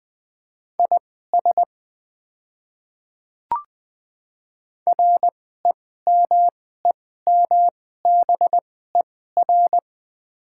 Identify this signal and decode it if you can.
Morse code, heard as a keyed tone, about 20 wpm, spaced as for 14 wpm, 690 Hz. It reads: IS REMEMBER
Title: 2x Character Spacing - Top 500 Words - 20wpm